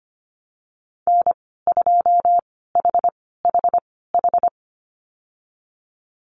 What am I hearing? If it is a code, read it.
Morse code, heard as a keyed tone, about 25 wpm, spaced as for 16 wpm, 700 Hz. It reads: N2HHH
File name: Lesson 15g - Number 5 - Call Signs 25wpm